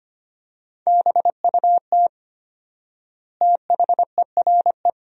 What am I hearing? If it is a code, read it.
Morse code, heard as a keyed tone, about 25 wpm, 700 Hz. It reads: BUT THERE
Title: 4x Word-Spacing - Sentences from Top 100 Words - 25wpm